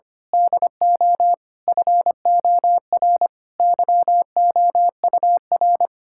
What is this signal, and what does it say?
Morse code, heard as a keyed tone, about 25 wpm, 700 Hz. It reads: DO FOR YOUR
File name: practice / 25wpm / Sentences with 9-Word Phrases - 25wpm